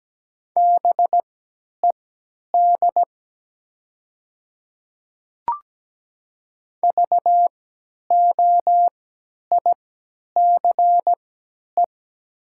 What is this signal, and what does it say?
Morse code, heard as a keyed tone, about 17 wpm, spaced as for 10 wpm, 700 Hz. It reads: BED VOICE
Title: 3x Character Spacing - Top 500 Words - 17wpm